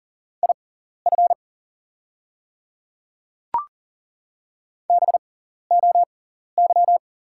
Morse code, heard as a keyed tone, about 40 wpm, spaced as for 14 wpm, 690 Hz. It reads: IF BOY
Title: 6x Character Spacing - Top 500 Words - 40wpm